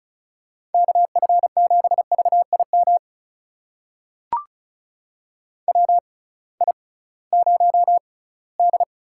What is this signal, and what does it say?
Morse code, heard as a keyed tone, about 36 wpm, spaced as for 20 wpm, 700 Hz. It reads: KF7VIM WI0D
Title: USA Call Signs - 6x Character Spacing - 35wpm